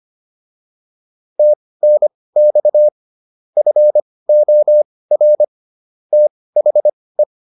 Morse code, heard as a keyed tone, about 25 wpm, spaced as for 18 wpm, 600 Hz. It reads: TNX FOR THE